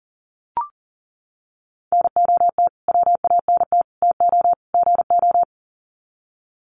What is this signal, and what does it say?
Morse code, heard as a keyed tone, about 40 wpm, 700 Hz. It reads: NOT WANT TO GO
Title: Sets of 4 Words 40wpm